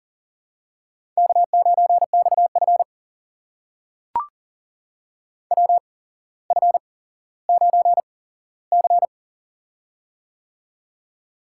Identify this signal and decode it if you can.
Morse code, heard as a keyed tone, about 40 wpm, spaced as for 17 wpm, 700 Hz. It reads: K9XF WF9C
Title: USA Call Signs - 4 Characters - 8x Character Spacing - 40wpm